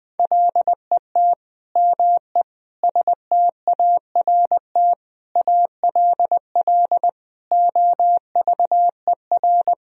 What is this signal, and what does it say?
Morse code, heard as a keyed tone, about 20 wpm, 700 Hz. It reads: LET ME START ALL OVER
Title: Sentences from Top 1000 Words - Review Part 1 - 20wpm